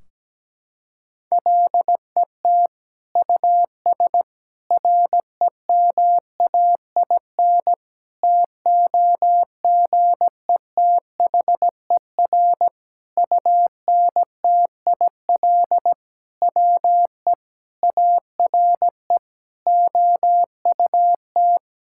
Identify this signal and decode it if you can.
Morse code, heard as a keyed tone, about 17 wpm, 700 Hz. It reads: LET US REMAIN TOGETHER UNTIL WE ARE OUT
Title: Sentences from Top 1600 Words 17wpm